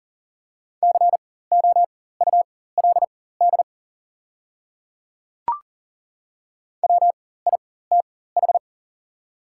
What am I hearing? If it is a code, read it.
Morse code, heard as a keyed tone, about 38 wpm, spaced as for 18 wpm, 690 Hz. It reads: COULD WITH
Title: 4x Character Spacing - Top 100 Words - 40wpm